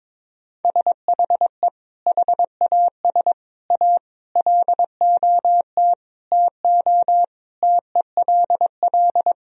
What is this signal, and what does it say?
Morse code, heard as a keyed tone, about 22 wpm, 700 Hz. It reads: SHE HAS A LOT TO TELL
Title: Sentences from Top 1000 Words - Review Part 3 - 22wpm